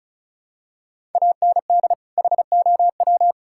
Morse code, heard as a keyed tone, about 36 wpm, 700 Hz. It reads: AND HOW
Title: Sentences from Top 1000 Words - Review Part 5 - 35wpm